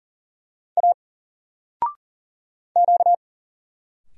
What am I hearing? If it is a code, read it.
Morse code, heard as a keyed tone, about 39 wpm, 700 Hz. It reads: A Q
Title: Single Letter-Number 40wpm